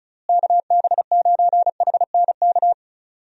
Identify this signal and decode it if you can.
Morse code, heard as a keyed tone, about 36 wpm, 700 Hz. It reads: KB9HNK
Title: USA Call Signs - 6 Characters - Repeated 3x - 35wpm